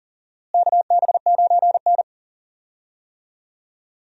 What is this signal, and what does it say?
Morse code, heard as a keyed tone, about 40 wpm, 700 Hz. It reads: KB9N
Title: USA Call Signs - 4 Characters - Repeated 2x - 40wpm